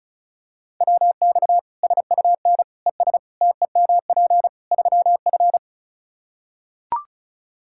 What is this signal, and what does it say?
Morse code, heard as a keyed tone, about 36 wpm, 700 Hz. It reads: WX SUN ES TEMP 3F
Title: Practice QSO - Exchange 2 - Info - 35wpm